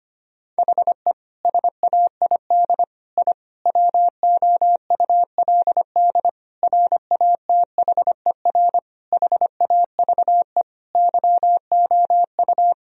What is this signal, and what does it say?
Morse code, heard as a keyed tone, about 25 wpm, 700 Hz. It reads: HE SAID I WOULD RATHER HAVE YOU
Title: Sentences from Top 1000 Words - Review Part 4 - 25wpm